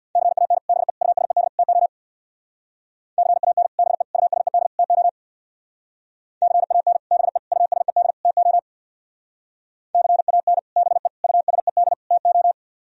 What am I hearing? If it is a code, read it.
Morse code, heard as a keyed tone, about 61 wpm, 700 Hz. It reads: CAN BE USED TO CAN BE USED TO CAN BE USED TO CAN BE USED TO
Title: Speed-Racing - Sets of 4 Words 50wpm